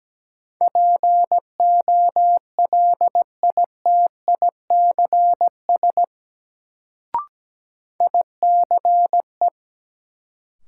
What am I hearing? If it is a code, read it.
Morse code, heard as a keyed tone, about 17 wpm, 700 Hz. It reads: POLITICS ICE
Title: Sentences from Top 1000 Words - New Word Drill - 17wpm